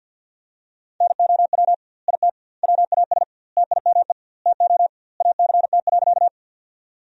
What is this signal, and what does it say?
Morse code, heard as a keyed tone, about 53 wpm, 700 Hz. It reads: NOW IT WAS TIME TO ACT.
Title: 2x Word-Spacing - Sentences from Top 200 Words - 50wpm